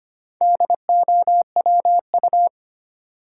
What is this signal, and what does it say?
Morse code, heard as a keyed tone, about 25 wpm, 700 Hz. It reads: DOWU
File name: practice / 25wpm / Four Letter-Number Combinations - 25wpm